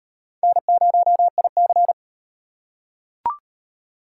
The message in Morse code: N0IC

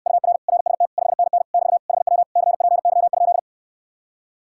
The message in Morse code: UR RST 5NN <BT> HW CPY?

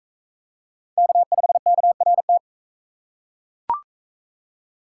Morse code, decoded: K5KRT

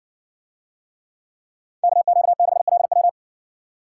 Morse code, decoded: KQ6LW

